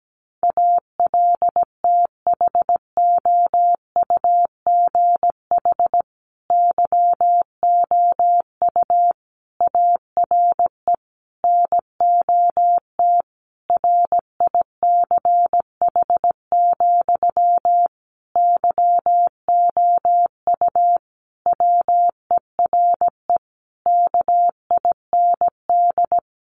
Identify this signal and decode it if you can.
Morse code, heard as a keyed tone, about 17 wpm, 700 Hz. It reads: ALTHOUGH YOU ARE NOT RICH, YOU WERE KIND
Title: Sentences from Top 1100 Words 17wpm